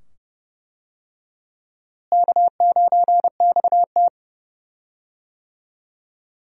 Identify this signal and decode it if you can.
Morse code, heard as a keyed tone, about 31 wpm, 700 Hz. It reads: K9XT
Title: USA Call Signs - 4 Characters - Repeated 2x - 30wpm